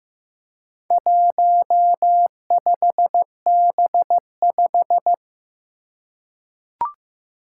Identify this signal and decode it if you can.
Morse code, heard as a keyed tone, about 15 wpm, 700 Hz. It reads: 15B5
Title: Four Challenging Letter-Number Combinations 15wpm